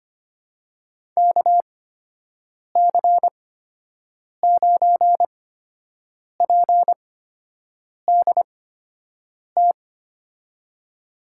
Morse code, heard as a keyed tone, about 25 wpm, spaced as for 7 wpm, 700 Hz. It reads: KC9PDT
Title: USA Call Signs - 6 Characters - 8x Character Spacing - 25wpm